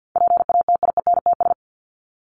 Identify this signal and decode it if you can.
Morse code, heard as a keyed tone, about 50 wpm, 700 Hz. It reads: PATIENTS
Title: ES:S Suffix - Top 100 Words with Base Words 50wpm